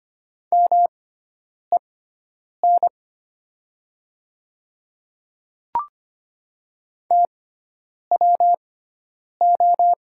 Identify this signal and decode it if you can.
Morse code, heard as a keyed tone, about 25 wpm, spaced as for 9 wpm, 700 Hz. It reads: MEN TWO
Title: 6x Character Spacing - Top 200 Words - 25wpm